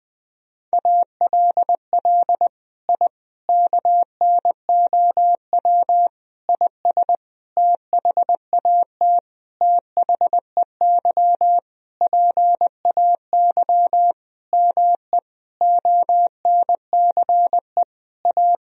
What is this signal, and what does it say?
Morse code, heard as a keyed tone, about 20 wpm, 700 Hz. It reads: ALL I KNOW IS THAT THEY PAY ME ONCE A
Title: Sentences from Top 1000 Words - Review Part 1 - 20wpm